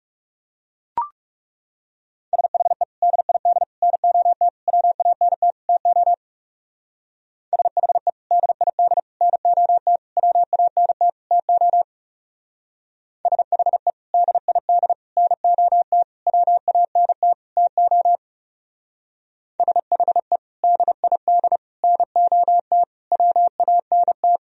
Morse code, heard as a keyed tone, about 37 wpm, 700 Hz. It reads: SHE DID NOT WANT TO SHE DID NOT WANT TO SHE DID NOT WANT TO SHE DID NOT WANT
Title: Speed-Racing - Sets of 5 Words 30wpm